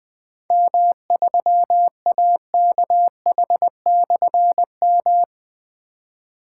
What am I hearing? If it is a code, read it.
Morse code, heard as a keyed tone, about 21 wpm, 700 Hz. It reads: M3AKH/M
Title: Difficult Call Signs 20wpm